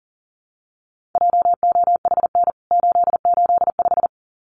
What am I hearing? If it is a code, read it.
Morse code, heard as a keyed tone, about 40 wpm, 700 Hz. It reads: JOHN 885
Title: CWT Exchange 40wpm